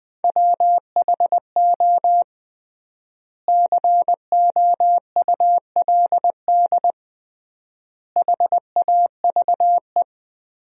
Morse code, heard as a keyed tone, about 21 wpm, 700 Hz. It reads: WHO COULD HAVE
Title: 3x Word-Spacing - Sentences from Top 300 Words - 20wpm